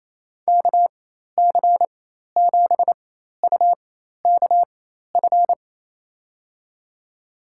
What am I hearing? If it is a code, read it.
Morse code, heard as a keyed tone, about 28 wpm, spaced as for 13 wpm, 700 Hz. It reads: KC7UKF
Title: USA Call Signs - 4x Character Spacing - 28wpm